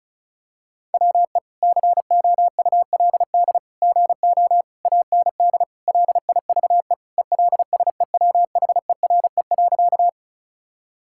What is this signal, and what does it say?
Morse code, heard as a keyed tone, about 34 wpm, 700 Hz. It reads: WE COULD GO AND LIVE ELSEWHERE.
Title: Sentences from Top 1600 Words 35wpm